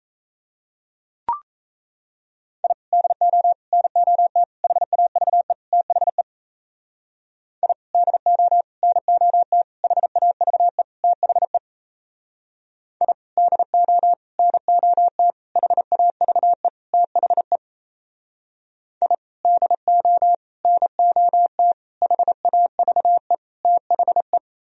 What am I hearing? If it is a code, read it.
Morse code, heard as a keyed tone, about 37 wpm, 700 Hz. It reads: I DO NOT HAVE THE I DO NOT HAVE THE I DO NOT HAVE THE I DO NOT HAVE THE